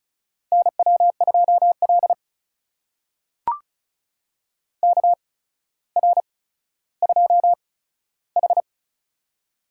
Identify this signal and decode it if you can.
Morse code, heard as a keyed tone, about 36 wpm, spaced as for 15 wpm, 700 Hz. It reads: NW2L KR2H
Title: USA Call Signs - 4 Characters - 8x Character Spacing - 35wpm